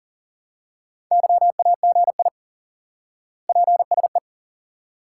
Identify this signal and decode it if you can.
Morse code, heard as a keyed tone, about 40 wpm, 700 Hz. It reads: YAGI PSE
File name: IQE Course 3 - Lesson 89 - YAGI - Two Elements - 40wpm